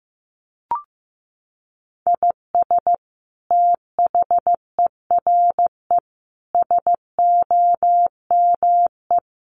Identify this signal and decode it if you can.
Morse code, heard as a keyed tone, about 15 wpm, 700 Hz. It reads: IS THERE SOME
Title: Sentences from Top 200 Words 15wpm